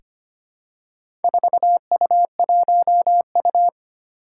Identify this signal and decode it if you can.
Morse code, heard as a keyed tone, about 25 wpm, 700 Hz. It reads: 4U1U
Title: Four Challenging Letter-Number Combinations 25wpm